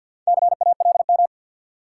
Morse code, heard as a keyed tone, about 52 wpm, 700 Hz. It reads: CALM
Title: Four Letter Words - 50wpm